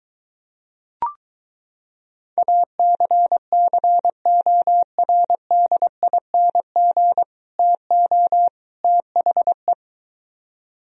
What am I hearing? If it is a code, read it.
Morse code, heard as a keyed tone, about 23 wpm, 700 Hz. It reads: ACCORDING TO THE